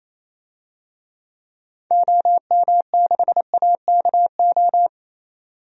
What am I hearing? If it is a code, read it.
Morse code, heard as a keyed tone, about 28 wpm, 700 Hz. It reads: OM6AKO
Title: DXCC-100 Call Signs - 6 Characters - 28wpm